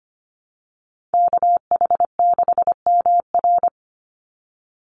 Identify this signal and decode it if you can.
Morse code, heard as a keyed tone, about 25 wpm, 700 Hz. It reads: KH6MR
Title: USA Call Signs 25wpm